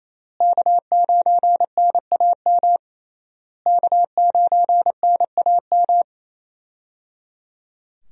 Morse code, heard as a keyed tone, about 28 wpm, 700 Hz. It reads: K9NAM K9NAM
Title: USA Call Signs - 5 Characters - Repeated 3x - 28wpm